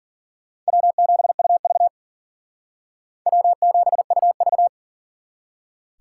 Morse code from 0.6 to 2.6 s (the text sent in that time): W7UV